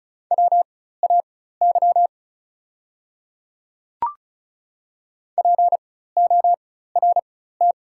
Morse code, heard as a keyed tone, about 36 wpm, spaced as for 16 wpm, 690 Hz. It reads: WAY PORT
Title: 4x Character Spacing - Top 300 Words - 35wpm